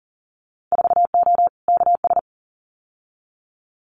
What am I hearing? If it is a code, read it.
Morse code, heard as a keyed tone, about 38 wpm, 700 Hz. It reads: 4O KS
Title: Winter Field Day Exchanges Encore 40wpm